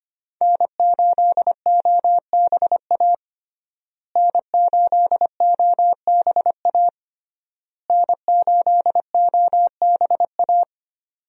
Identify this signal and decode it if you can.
Morse code, heard as a keyed tone, about 25 wpm, 700 Hz. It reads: N8OBA N8OBA N8OBA
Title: USA Call Signs - 5 Characters - Repeated 3x - 25wpm